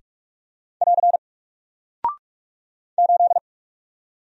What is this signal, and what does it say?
Morse code, heard as a keyed tone, about 44 wpm, 700 Hz. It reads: <AR> 8
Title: Single Character (Letter:Number:Punctuation:Prosign) 45wpm